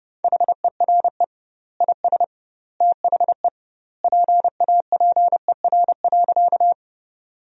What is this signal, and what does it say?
Morse code, heard as a keyed tone, about 30 wpm, 700 Hz. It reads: HERE IS THE PAPER.
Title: 2x Word-Spacing - Sentences from Top 300 Words - 30wpm